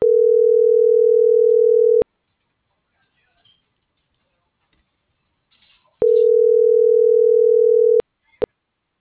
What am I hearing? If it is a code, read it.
Ringback tone (North America), 440 + 480 Hz.